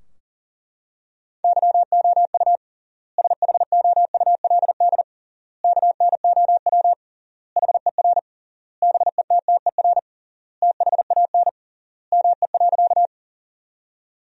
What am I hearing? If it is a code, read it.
Morse code, heard as a keyed tone, about 40 wpm, 700 Hz. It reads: YOU SHOULD KNOW HER BETTER THAN ME.
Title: 3x Word-Spacing - Sentences from Top 400 Words - 40wpm